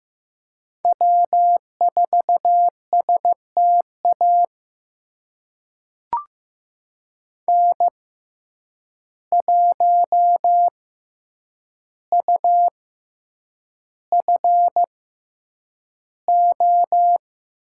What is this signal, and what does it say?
Morse code, heard as a keyed tone, about 15 wpm, spaced as for 8 wpm, 700 Hz. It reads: W4STA N1UFO
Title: USA Call Signs - 5 Characters - 6x Character Spacing - 15wpm